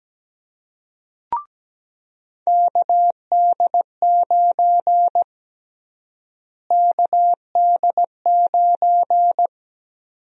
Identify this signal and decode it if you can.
Morse code, heard as a keyed tone, about 17 wpm, 700 Hz. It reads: KD9 KD9
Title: USA Call Sign Prefixes - Repeated 2x - 17wpm